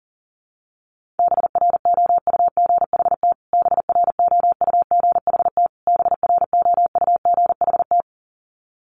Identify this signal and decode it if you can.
Morse code, heard as a keyed tone, about 40 wpm, 700 Hz. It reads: BROUGHT BROUGHT BROUGHT
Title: Top 500 Words Repeated 40wpm